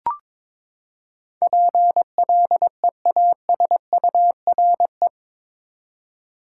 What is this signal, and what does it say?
Morse code, heard as a keyed tone, about 22 wpm, 700 Hz. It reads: PLEASURE
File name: practice / 22wpm / Eight Letter Words - 22wpm